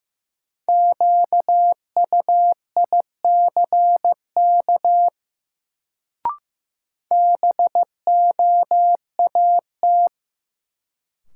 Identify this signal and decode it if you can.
Morse code, heard as a keyed tone, about 15 wpm, 700 Hz. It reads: QUICK BOAT
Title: Sentences from Top 500 Words - New Word Drill - 15wpm